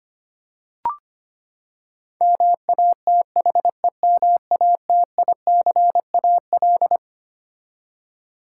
Morse code, heard as a keyed tone, about 25 wpm, 700 Hz. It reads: MATHEMATICAL
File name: Twelve Letter Words - 25wpm